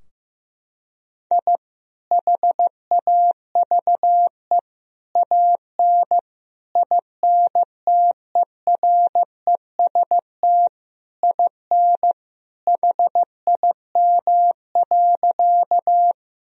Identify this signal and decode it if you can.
Morse code, heard as a keyed tone, about 16 wpm, 700 Hz. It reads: I HAVE AN INTEREST IN HIM.
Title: Sentences from Top 1000 Words - Review Part 2 - 15wpm